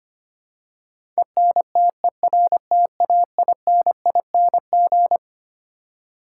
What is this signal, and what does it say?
Morse code, heard as a keyed tone, about 25 wpm, 700 Hz. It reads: ENTERTAINING
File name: Twelve Letter Words - 25wpm